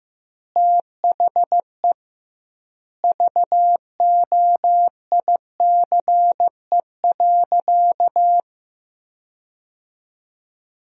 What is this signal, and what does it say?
Morse code, heard as a keyed tone, about 16 wpm, 700 Hz. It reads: THE VOICE.
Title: 2x Word-Spacing - Sentences from Top 500 Words - 15wpm